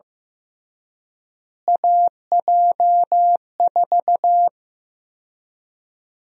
Morse code, heard as a keyed tone, about 15 wpm, 700 Hz. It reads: AJ4
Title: USA Call Sign Prefixes - Repeated 2x - 15wpm